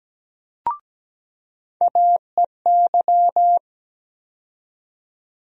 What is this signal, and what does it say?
Morse code, heard as a keyed tone, about 17 wpm, 700 Hz. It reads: AEY
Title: USA Call Sign Postfixes - 3 Letters 17wpm